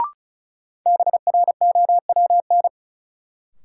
Morse code, heard as a keyed tone, about 34 wpm, 700 Hz. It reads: BROWN
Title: Sentences from Top 1200 Words - New Word Drill - 35wpm